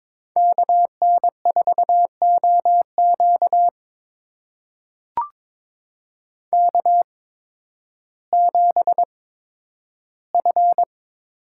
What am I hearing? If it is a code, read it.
Morse code, heard as a keyed tone, about 22 wpm, spaced as for 12 wpm, 700 Hz. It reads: KN4OQ K7F
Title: USA Call Signs - 5 Characters - 8x Character Spacing - 22wpm